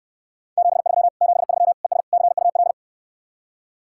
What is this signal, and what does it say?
Morse code, heard as a keyed tone, about 65 wpm, 700 Hz. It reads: 73 73 ES CUL